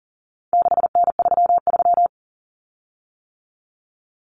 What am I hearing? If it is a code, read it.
Morse code, heard as a keyed tone, about 39 wpm, 700 Hz. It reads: 6N33
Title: Common Cut Numbers - 4-Digit 40wpm